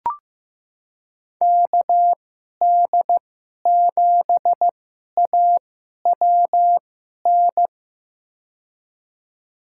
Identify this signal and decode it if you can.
Morse code, heard as a keyed tone, about 15 wpm, spaced as for 11 wpm, 700 Hz. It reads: KD7AWN
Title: USA Call Signs - 6 Characters - 2x Character Spacing - 15wpm